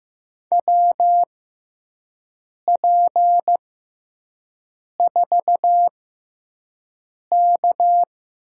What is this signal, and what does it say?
Morse code, heard as a keyed tone, about 15 wpm, spaced as for 5 wpm, 700 Hz. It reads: WP4K
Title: USA Call Signs - 6x Character Spacing - 15wpm